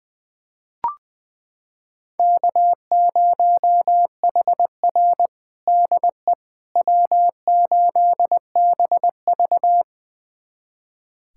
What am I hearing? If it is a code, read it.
Morse code, heard as a keyed tone, about 20 wpm, 700 Hz. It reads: K0HR DE W8BV
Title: USA Callsign DE Callsign - 4 Characters - 20wpm